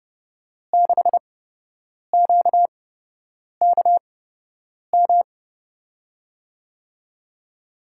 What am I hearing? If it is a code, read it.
Morse code, heard as a keyed tone, about 30 wpm, spaced as for 8 wpm, 700 Hz. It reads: 6QKM
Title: USA Call Signs - 6 Characters - 8x Character Spacing - 30wpm